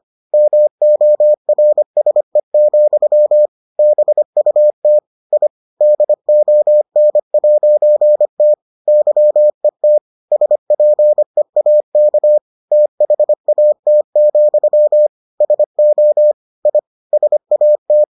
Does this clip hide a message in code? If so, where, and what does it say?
Yes: Morse code from 0.3 to 18.0 s, heard as a keyed tone, about 25 wpm, 600 Hz. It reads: MORSE, BUT I DON'T YET SPEAK THAT, SO I SAT